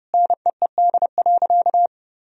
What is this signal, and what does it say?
Morse code, heard as a keyed tone, about 30 wpm, 700 Hz. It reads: NEED.
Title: Sentences from Top 1000 Words - Review Part 3 - 30wpm